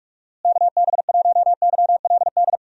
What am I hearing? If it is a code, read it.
Morse code, heard as a keyed tone, about 46 wpm, 700 Hz. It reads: KB1YLD
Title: USA Call Signs - 6 Characters - Repeated 2x - 45wpm